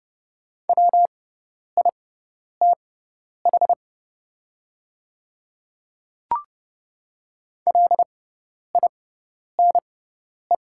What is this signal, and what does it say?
Morse code, heard as a keyed tone, about 30 wpm, spaced as for 10 wpm, 700 Hz. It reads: WITH LINE
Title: 6x Character Spacing - Top 500 Words - 30wpm